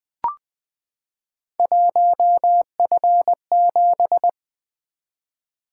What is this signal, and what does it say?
Morse code, heard as a keyed tone, about 20 wpm, 700 Hz. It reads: 1F7
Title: Three Letter-Number Combinations - 20wpm